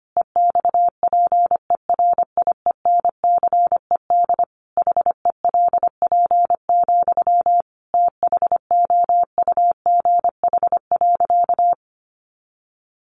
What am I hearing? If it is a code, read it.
Morse code, heard as a keyed tone, about 25 wpm, 690 Hz. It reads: EXPERIENCED HELP, THOUGH.
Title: Sentences from Top 1100 Words 25wpm